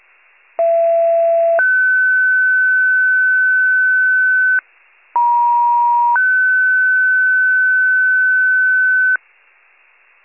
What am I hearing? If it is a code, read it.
Two-tone page, heard as a one-second tone, then a three-second tone, about 670 then 1500 Hz, about 950 then 1500 Hz. It is heard from 0.6 to 9.2 s.
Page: 39 FIRE